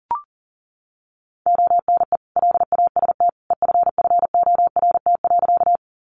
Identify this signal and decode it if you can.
Morse code, heard as a keyed tone, about 40 wpm, 700 Hz. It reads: ONE LAST EFFORT.
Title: Sentences from Top 1200 Words 40wpm